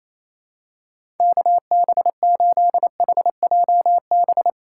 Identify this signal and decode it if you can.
Morse code, heard as a keyed tone, about 28 wpm, 700 Hz. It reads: KB8HJB
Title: USA Call Signs - 6 Characters - 4x Character Spacing - 28wpm